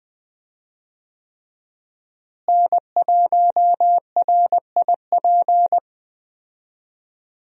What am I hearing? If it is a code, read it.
Morse code, heard as a keyed tone, about 20 wpm, 700 Hz. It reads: N1RIP
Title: USA Call Signs - 5 Characters - Repeated 3x - 20wpm